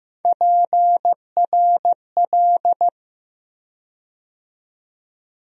Morse code, heard as a keyed tone, about 15 wpm, 700 Hz. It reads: PRL
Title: USA Call Sign Postfixes - 3 Letters 15wpm